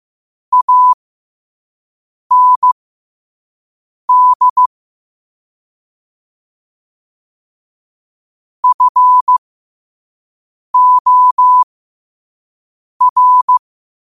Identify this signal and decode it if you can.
Morse code, heard as a keyed tone, about 15 wpm, spaced as for 5 wpm, 1000 Hz. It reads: AND FOR
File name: qotdfiles / 5wpm